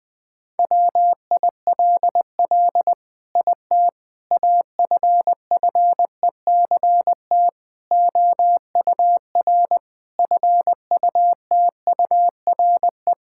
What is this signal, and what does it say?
Morse code, heard as a keyed tone, about 20 wpm, 700 Hz. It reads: WILL IT AFFECT OUR FUTURE
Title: Sentences from Top 1000 Words - Review Part 6 - 20wpm